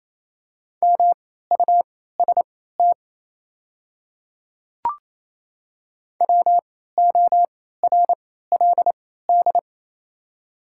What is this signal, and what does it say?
Morse code, heard as a keyed tone, about 28 wpm, spaced as for 16 wpm, 690 Hz. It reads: MUST WORLD
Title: 3x Character Spacing - Top 500 Words - 28wpm